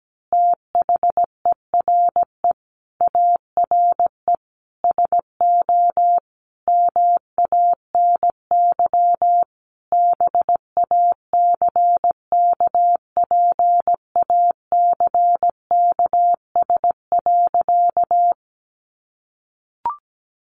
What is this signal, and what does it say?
Morse code, heard as a keyed tone, about 17 wpm, 700 Hz. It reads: THERE ARE SO MANY BACKPACKS.
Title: Additional Easy Phrases 17wpm